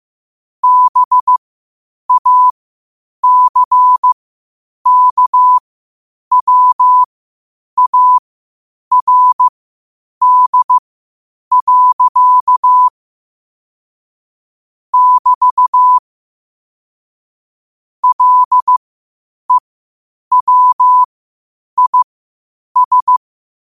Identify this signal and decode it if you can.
Morse code, heard as a keyed tone, about 15 wpm, spaced as for 8 wpm, 1000 Hz. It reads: BACKWARD. <BT> LEWIS
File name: qotdfiles / 8wpm